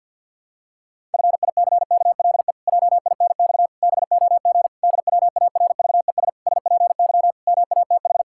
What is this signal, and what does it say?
Morse code, heard as a keyed tone, about 50 wpm, 700 Hz. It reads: FICKLE JINX BOG DWARVES SPY MATH
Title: Pangram Challenge v2 - 50wpm